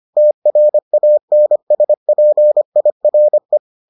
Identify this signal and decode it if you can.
Morse code, heard as a keyed tone, about 25 wpm, 600 Hz. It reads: TRANSPIRE